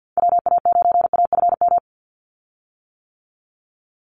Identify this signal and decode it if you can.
Morse code, heard as a keyed tone, about 50 wpm, 700 Hz. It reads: WA9AFM
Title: USA Call Signs 50wpm